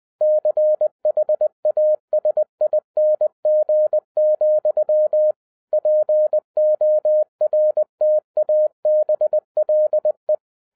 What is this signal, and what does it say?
Morse code, heard as a keyed tone, about 20 wpm, 610 Hz. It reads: CHASING, PORTABLE
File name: 2011-10-mp3 / n13-TIPS FROM MY 34 YEARS OF EXPERIENCE 20wpm